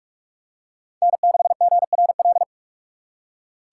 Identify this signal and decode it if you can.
Morse code, heard as a keyed tone, about 44 wpm, 700 Hz. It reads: N6GRL
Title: USA Call Signs - 6x Character Spacing - 45wpm